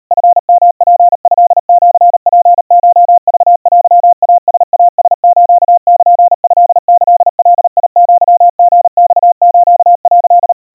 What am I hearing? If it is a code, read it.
Morse code (Japanese Wabun code), heard as a keyed tone, about 38 wpm, 700 Hz. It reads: チヨツトシツコクテイライラスルトキガアリマス」
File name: kawa / Morse sample / kamyu 140 cpm 700 Hz Crispy